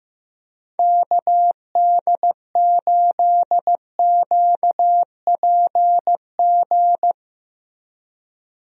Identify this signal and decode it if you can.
Morse code, heard as a keyed tone, about 15 wpm, 700 Hz. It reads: KD8QPG
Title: USA Call Signs - 6 Characters - Repeated 3x - 15wpm